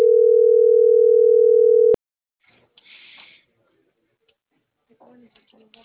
Ringback tone (North America), 440 + 480 Hz.